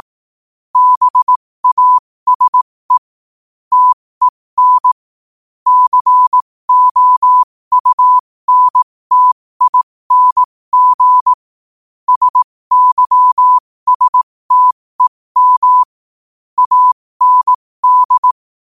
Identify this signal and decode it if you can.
Morse code, heard as a keyed tone, about 18 wpm, spaced as for 15 wpm, 1000 Hz. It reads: BASE TEN COUNTING SYSTEM AND
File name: qotdfiles / 15wpm